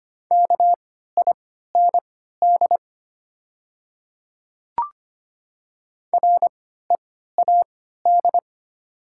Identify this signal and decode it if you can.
Morse code, heard as a keyed tone, about 25 wpm, spaced as for 14 wpm, 700 Hz. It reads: KIND READ
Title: 3x Character Spacing - Top 200 Words - 25wpm